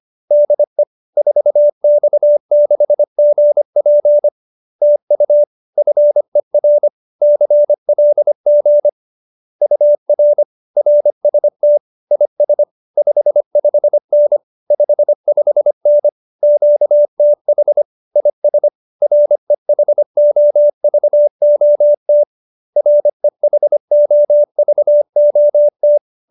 Morse code, heard as a keyed tone, about 25 wpm, 590 Hz. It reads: DE 4X6GP TU FER CLG UR RST IS 55N 55N QTH IS REHOVOT REHOVOT
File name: QSO315 25